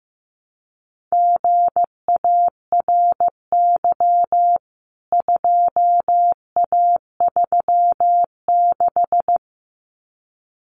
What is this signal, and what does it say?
Morse code, heard as a keyed tone, about 15 wpm, 700 Hz. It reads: GARY 2A36
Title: CWT Exchange 15wpm